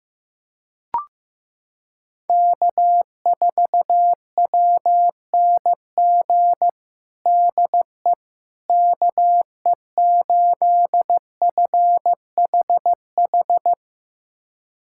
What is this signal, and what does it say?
Morse code, heard as a keyed tone, about 15 wpm, 700 Hz. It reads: K4WNG DE KE8FHH
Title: USA Callsign DE Callsign 15wpm